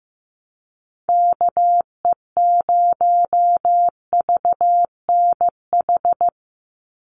Morse code, heard as a keyed tone, about 15 wpm, 700 Hz. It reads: KE0VNH
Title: USA Call Signs 15wpm